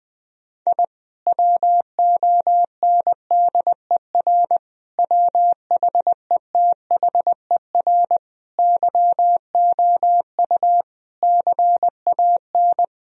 Morse code, heard as a keyed tone, about 20 wpm, 700 Hz. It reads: I WONDER WHETHER YOU CAN
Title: Sentences from Top 1000 Words - Review Part 3 - 20wpm